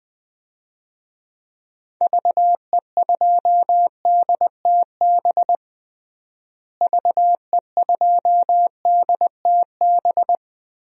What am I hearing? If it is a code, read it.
Morse code, heard as a keyed tone, about 21 wpm, 700 Hz. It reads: VE2DTB VE2DTB
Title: DXCC-100 Call Signs - 6 Characters - Repeated 3x - 20wpm